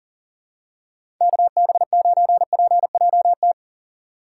Morse code, heard as a keyed tone, about 39 wpm, 700 Hz. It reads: KB9PJT
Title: USA Call Signs - 6 Characters - 3x Character Spacing - 40wpm